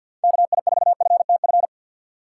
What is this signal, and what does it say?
Morse code, heard as a keyed tone, about 50 wpm, 700 Hz. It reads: KI4FTF
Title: USA Call Signs - 6 Characters - 6x Character Spacing - 50wpm